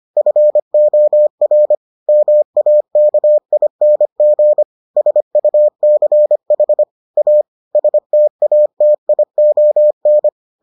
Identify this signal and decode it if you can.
Morse code, heard as a keyed tone, about 25 wpm, 600 Hz. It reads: FOR MAKING SUCH A STATION